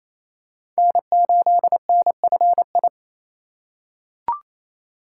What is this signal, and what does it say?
Morse code, heard as a keyed tone, about 28 wpm, 700 Hz. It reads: N8NFI
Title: USA Call Signs - 3x Character Spacing - 28wpm